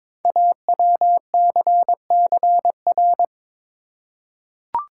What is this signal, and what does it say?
Morse code, heard as a keyed tone, about 22 wpm, 700 Hz. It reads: AWCCR
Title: Code Groups - 5 Letters - 22wpm